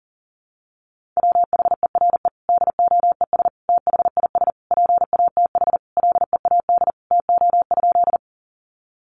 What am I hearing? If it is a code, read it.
Morse code, heard as a keyed tone, about 40 wpm, 700 Hz. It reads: WHERE DOES THIS PATH LEAD TO?
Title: Sentences from Top 1400 Words 40wpm